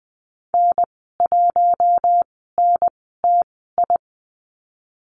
Morse code, heard as a keyed tone, about 20 wpm, spaced as for 15 wpm, 700 Hz. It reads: N1NTI
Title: 055 Lesson 10g - Letter r - Call Signs 20wpm